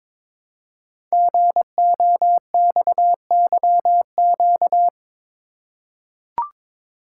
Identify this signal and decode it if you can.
Morse code, heard as a keyed tone, about 23 wpm, 700 Hz. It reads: GOXYQ
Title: Code Groups - 5 Letter:Numbers - 22wpm